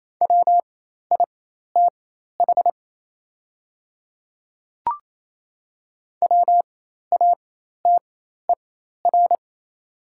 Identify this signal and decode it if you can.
Morse code, heard as a keyed tone, about 28 wpm, spaced as for 13 wpm, 700 Hz. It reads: WITH WATER
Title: 4x Character Spacing - Top 200 Words - 28wpm